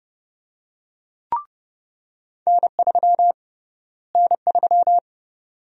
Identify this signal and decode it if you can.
Morse code, heard as a keyed tone, about 29 wpm, 700 Hz. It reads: N3 N3
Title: USA Call Sign Prefixes - Repeated 2x - 30wpm